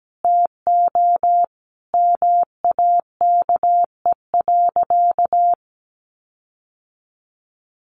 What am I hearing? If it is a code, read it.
Morse code, heard as a keyed tone, about 17 wpm, 700 Hz. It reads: TO MAKE.
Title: Sentences from Top 1400 Words 17wpm